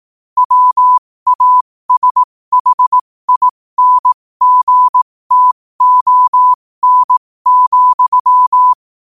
Morse code, heard as a keyed tone, about 18 wpm, spaced as for 15 wpm, 1000 Hz. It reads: WASHINGTON,
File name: qotdfiles / 15wpm